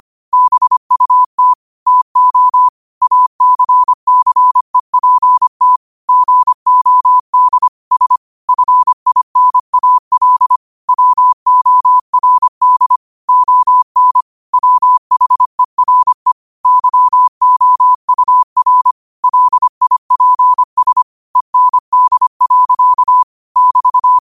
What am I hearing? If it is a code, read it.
Morse code, heard as a keyed tone, about 25 wpm, 1000 Hz. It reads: BUT TO ACCEPT GOD S FINAL WORD ON WHERE YOUR LIPS END. <BT>
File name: qotdfiles / 25wpm